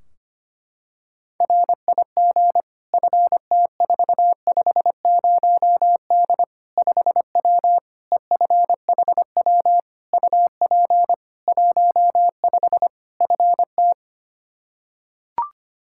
Morse code, heard as a keyed tone, about 25 wpm, 700 Hz. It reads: RIG FT450D 5W EFHW UP 15 FT
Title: Practice QSO - Exchange 2 - Info - 25wpm